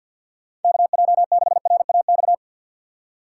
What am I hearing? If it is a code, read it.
Morse code, heard as a keyed tone, about 52 wpm, 700 Hz. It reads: KJ6RAX
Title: USA Call Signs - 6 Characters - 6x Character Spacing - 50wpm